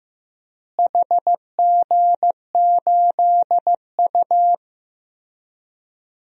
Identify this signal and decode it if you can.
Morse code, heard as a keyed tone, about 16 wpm, 700 Hz. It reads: HG8U
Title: DXCC-100 Call Signs - 4 Characters - Repeated 3x - 15wpm